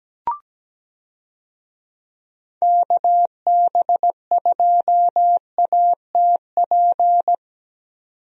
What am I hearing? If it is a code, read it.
Morse code, heard as a keyed tone, about 17 wpm, 700 Hz. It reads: KB2ATP
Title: USA Call Signs - 6 Characters - 17wpm